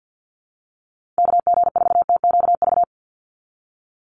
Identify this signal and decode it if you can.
Morse code, heard as a keyed tone, about 50 wpm, 700 Hz. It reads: KG4TQV
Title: USA Call Signs 50wpm